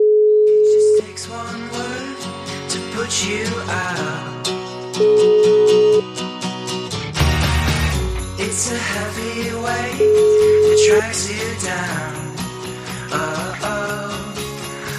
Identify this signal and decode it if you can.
Ringback tone (Europe, 425 Hz).